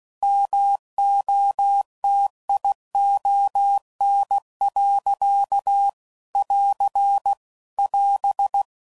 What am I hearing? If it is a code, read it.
Morse code, heard as a keyed tone, about 16 wpm, 800 Hz. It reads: MOTION. <AR> <AS>